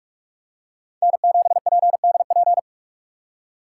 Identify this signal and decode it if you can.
Morse code, heard as a keyed tone, about 47 wpm, 700 Hz. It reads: N7PDP
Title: USA Call Signs - 2x Character Spacing - 45wpm